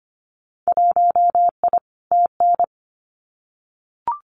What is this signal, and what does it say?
Morse code, heard as a keyed tone, about 25 wpm, 700 Hz. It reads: 1I TN
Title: Winter Field Day Exchanges 25wpm